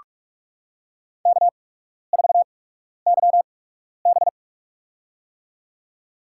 Morse code, heard as a keyed tone, about 44 wpm, spaced as for 12 wpm, 690 Hz. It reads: K4YB
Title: USA Call Signs - 4 Characters - 8x Character Spacing - 45wpm